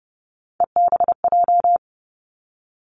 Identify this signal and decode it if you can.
Morse code, heard as a keyed tone, about 30 wpm, 700 Hz. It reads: EBJ
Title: Three Challenging Letter-Number Combinations 30wpm